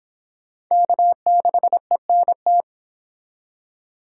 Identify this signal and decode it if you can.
Morse code, heard as a keyed tone, about 26 wpm, 700 Hz. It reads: K6ENT